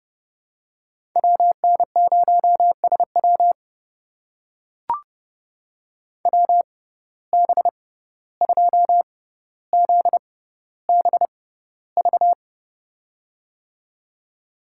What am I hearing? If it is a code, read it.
Morse code, heard as a keyed tone, about 30 wpm, spaced as for 15 wpm, 700 Hz. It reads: WN0SW WB2ZBV
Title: USA Call Signs - 6x Character Spacing - 30wpm